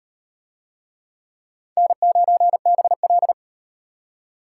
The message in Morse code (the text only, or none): N9BL